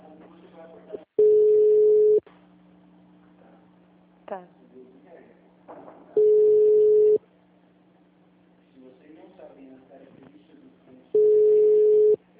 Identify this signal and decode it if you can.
Ringback tone (Europe, 425 Hz), with speech.
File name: out- -2014- - - .0